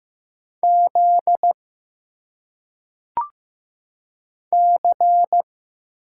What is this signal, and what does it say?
Morse code, heard as a keyed tone, about 15 wpm, 700 Hz. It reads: Z C
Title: Single Letters - Phonetic Pronunciation 15wpm